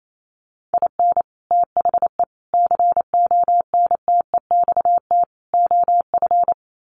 Morse code, heard as a keyed tone, about 28 wpm, 700 Hz. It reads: IN THE CONTEXT OF
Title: Sets of 4 Words 28wpm